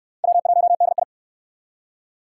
Morse code, heard as a keyed tone, about 57 wpm, 700 Hz. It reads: W1DI